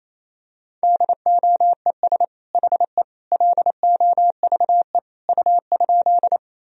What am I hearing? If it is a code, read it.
Morse code, heard as a keyed tone, about 28 wpm, 700 Hz. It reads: DOES HE LOVE U?
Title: Sentences from Top 400 Words and Abbreviations 28wpm